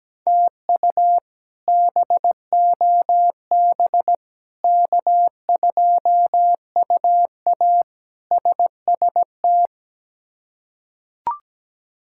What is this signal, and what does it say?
Morse code, heard as a keyed tone, about 17 wpm, 700 Hz. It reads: TU BOB K2UA SST
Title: K1USN SST QSO Practice - 17wpm